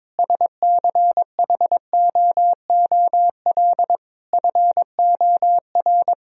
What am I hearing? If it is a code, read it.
Morse code, heard as a keyed tone, about 21 wpm, 700 Hz. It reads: SCHOOL FOR
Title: Sentences from Top 1000 Words - Review Part 4 - 22wpm